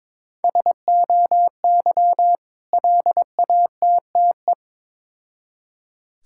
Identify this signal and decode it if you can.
Morse code, heard as a keyed tone, about 22 wpm, 700 Hz. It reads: SOY LATTE
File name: Difficult Words with Hints - 22wpm